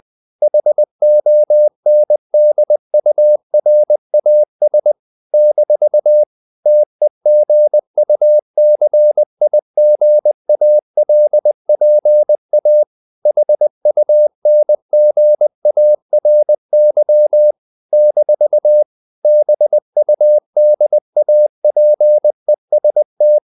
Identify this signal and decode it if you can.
Morse code, heard as a keyed tone, about 20 wpm, 600 Hz. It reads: HONDURAS - TEGUCIGALPA HUNGARY - BUDAPEST
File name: CW / Practice / NationalCapitals20